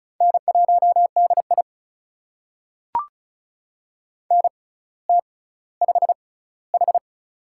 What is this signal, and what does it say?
Morse code, heard as a keyed tone, about 36 wpm, spaced as for 18 wpm, 700 Hz. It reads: N1DI NT5H
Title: USA Call Signs - 4 Characters - 6x Character Spacing - 35wpm